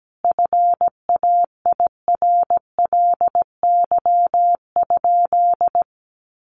Morse code, heard as a keyed tone, about 17 wpm, 700 Hz. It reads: FAIRLY?
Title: Sentences from Top 1100 Words 17wpm